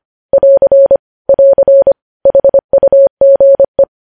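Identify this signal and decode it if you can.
Morse code, heard as a keyed tone, about 25 wpm, 550 Hz. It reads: <AR> <AR> HUGE
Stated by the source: News headlines sent in Morse at 25.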